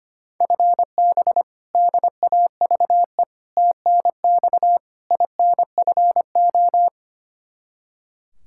Morse code, heard as a keyed tone, about 25 wpm, 700 Hz. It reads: FB DAVE TNX INFO
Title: K1USN SST QSO Practice 25wpm